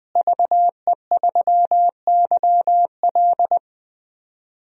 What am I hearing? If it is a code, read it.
Morse code, heard as a keyed tone, about 21 wpm, 700 Hz. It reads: VE3YL
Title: DXCC-100 Call Signs - 5 Characters - Repeated 3x - 20wpm